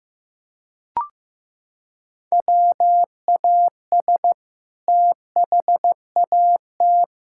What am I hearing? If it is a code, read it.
Morse code, heard as a keyed tone, about 15 wpm, 700 Hz. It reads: WAS THAT
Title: Sentences from Top 1600 Words 15wpm